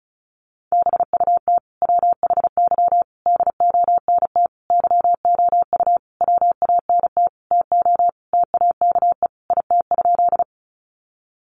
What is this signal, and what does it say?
Morse code, heard as a keyed tone, about 34 wpm, 700 Hz. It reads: BUT WHY DONT YOU WANT TO TAKE IT?
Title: Sentences from Top 300 Words 35wpm